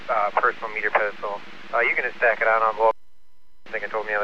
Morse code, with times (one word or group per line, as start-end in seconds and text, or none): none